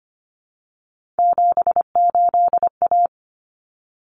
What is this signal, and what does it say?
Morse code, heard as a keyed tone, about 25 wpm, 700 Hz. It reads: 78A
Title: Common Cut Numbers - 3-Digit 25wpm